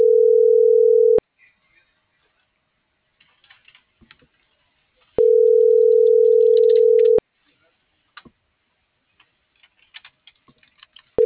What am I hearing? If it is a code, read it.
Ringback tone (North America), 440 + 480 Hz.